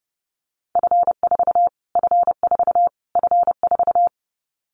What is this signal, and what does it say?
Morse code, heard as a keyed tone, about 30 wpm, 700 Hz. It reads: F4 F4 F4
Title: Callsign Prefixes for Top 25 DXCC - Repeated 3x - 30wpm